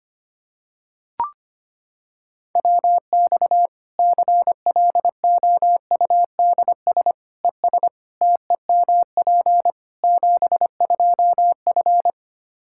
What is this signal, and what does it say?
Morse code, heard as a keyed tone, about 25 wpm, 700 Hz. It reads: WX CLOUDS ES TEMP 72F
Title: Practice QSO - Exchange 2 - Info - 25wpm